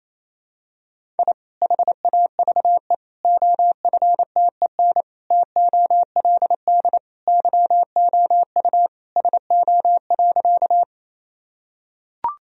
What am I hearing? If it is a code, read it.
Morse code, heard as a keyed tone, about 28 wpm, 700 Hz. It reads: I HAVE OFTEN TOLD YOU SO.
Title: Sentences from Top 1000 Words - Review Part 1 - 28wpm